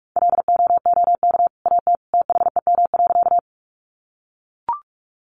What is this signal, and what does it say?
Morse code, heard as a keyed tone, about 44 wpm, 700 Hz. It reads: LOOK AT THEM.
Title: Sentences from Top 100 Words 45wpm